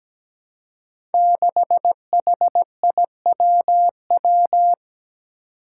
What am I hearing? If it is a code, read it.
Morse code, heard as a keyed tone, about 17 wpm, 700 Hz. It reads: 6HIWW
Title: Five Challenging Letter-Number Combinations 17wpm